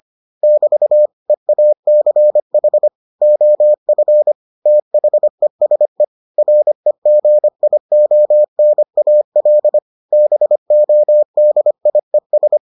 Morse code, heard as a keyed tone, about 25 wpm, 600 Hz. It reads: <BT> EACH OF THESE REGIONAL BODIES